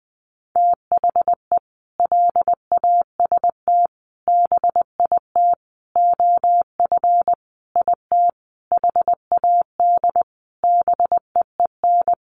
Morse code, heard as a keyed tone, about 20 wpm, 700 Hz. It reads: THE LAST BIT OF IT HAD BEEN
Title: Sentences from Top 900 Words 20wpm